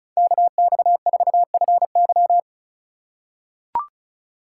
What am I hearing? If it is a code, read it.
Morse code, heard as a keyed tone, about 34 wpm, 700 Hz. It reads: KX4FY
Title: USA Call Signs - 5 Characters - 4x Character Spacing - 35wpm